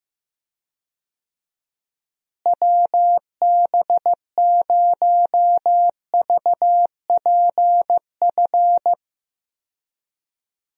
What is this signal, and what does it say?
Morse code, heard as a keyed tone, about 15 wpm, 700 Hz. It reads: WB0VPF
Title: USA Call Signs - 6 Characters - Repeated 3x - 15wpm